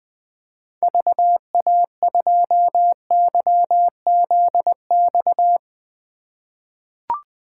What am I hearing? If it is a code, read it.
Morse code, heard as a keyed tone, about 20 wpm, 700 Hz. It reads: VA2YZX
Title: 2023 SOTA Chaser Call Signs - 20wpm